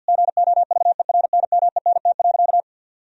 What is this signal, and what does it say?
Morse code, heard as a keyed tone, about 52 wpm, 700 Hz. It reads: GOVERNMENT.
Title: Sentences from Top 1000 Words - Review Part 3 - 50wpm